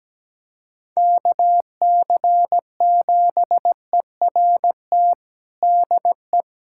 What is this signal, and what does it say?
Morse code, heard as a keyed tone, about 17 wpm, 700 Hz. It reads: KC7ERT DE
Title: USA Callsign DE Callsign - 6 Characters - 17wpm